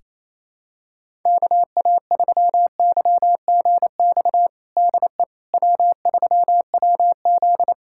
Morse code, heard as a keyed tone, about 28 wpm, 700 Hz. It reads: KA3YGX DE W3WZ
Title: USA Callsign DE Callsign 28wpm